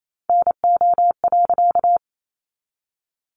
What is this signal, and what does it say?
Morse code, heard as a keyed tone, about 28 wpm, 700 Hz. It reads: NO.
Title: A Sound Of Thunder 28wpm